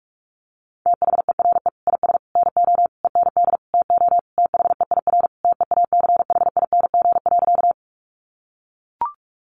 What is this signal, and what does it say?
Morse code, heard as a keyed tone, about 44 wpm, 700 Hz. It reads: THERE IS NO END TO THEIR TEACHING.
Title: Sentences from Top 1100 Words 45wpm